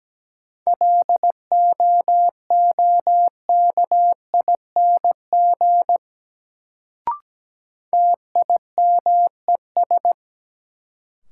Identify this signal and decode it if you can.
Morse code, heard as a keyed tone, about 17 wpm, 700 Hz. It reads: LOOKING TIMES
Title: Sentences from Top 1000 Words - Review Part 6 - New Word Drill - 17wpm